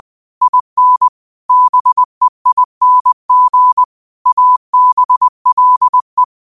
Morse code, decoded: IN BEING ABLE

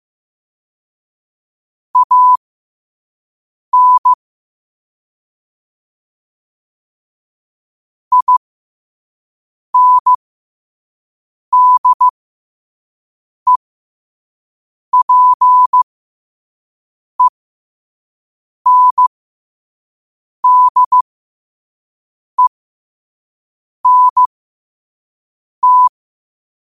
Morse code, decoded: AN INDEPENDENT

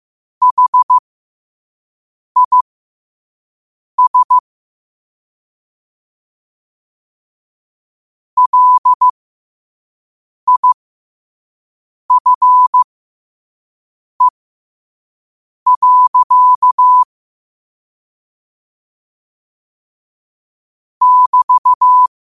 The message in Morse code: HIS LIFE. <BT>

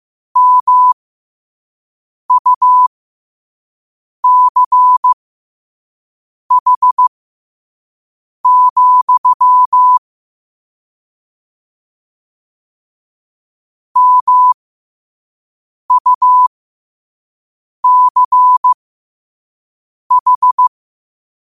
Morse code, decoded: MUCH, MUCH

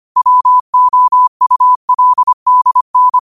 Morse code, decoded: WOULDN

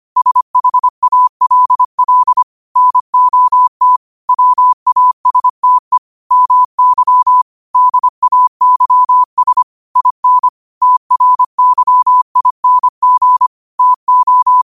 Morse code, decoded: SHALL NOT WASTE MY DAYS IN TRYING TO

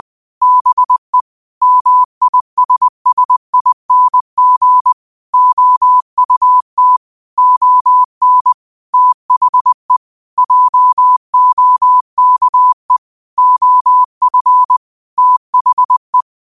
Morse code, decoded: BE MISSING OUT ON THE JOKE OF THE

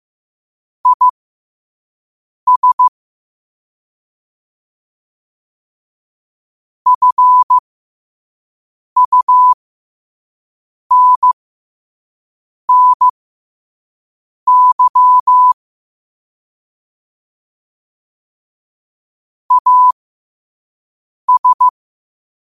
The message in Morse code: IS FUNNY AS

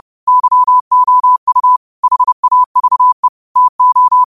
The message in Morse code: YOU HAVE TO

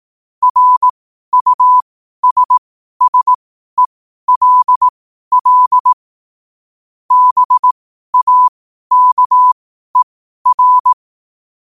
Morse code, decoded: RUSSELL BAKER

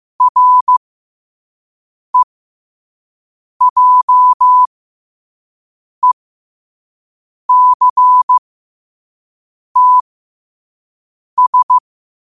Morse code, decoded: REJECTS

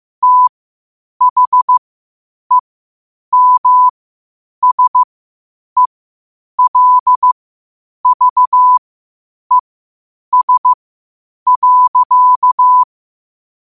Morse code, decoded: THEMSELVES.

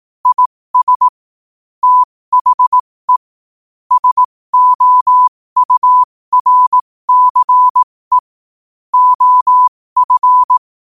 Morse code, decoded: IS THE SOURCE OF

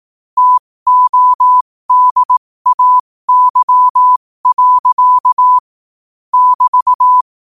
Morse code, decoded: TODAY. <BT>

